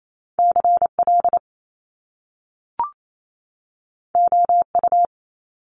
Morse code, heard as a keyed tone, about 28 wpm, 700 Hz. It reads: CL OU
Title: Top Two Letter Combinations in Words 28wpm